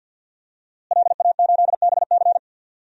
Morse code, heard as a keyed tone, about 52 wpm, 700 Hz. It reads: LA8BC
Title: DXCC-100 Call Signs - 5 Characters - Repeated 2x - 50wpm